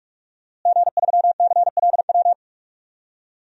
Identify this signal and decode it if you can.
Morse code, heard as a keyed tone, about 41 wpm, 700 Hz. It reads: G3CLW
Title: DXCC-100 Call Signs - 5 Characters - Repeated 3x - 45wpm